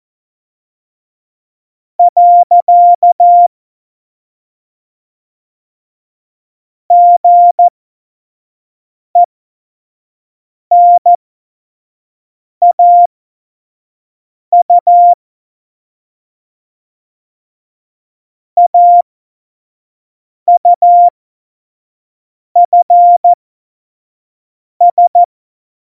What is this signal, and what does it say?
Morse code, heard as a keyed tone, about 14 wpm, spaced as for 5 wpm, 700 Hz. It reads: . GENAU AUFS
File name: Morsekurs / Witze / Witz 15. Mai 2018, 25 F70